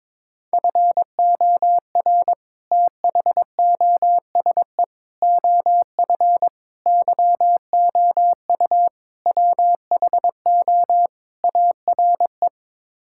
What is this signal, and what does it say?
Morse code, heard as a keyed tone, about 22 wpm, 700 Hz. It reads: FOR THOSE OF YOU WHO ARE
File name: practice / 22wpm / Sets of 6 Words - 22wpm